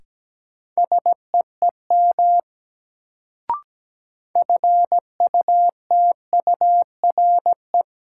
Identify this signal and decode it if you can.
Morse code, heard as a keyed tone, about 17 wpm, 700 Hz. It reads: SEEM FUTURE
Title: Sets of 3 Words - New Word Drill - 17wpm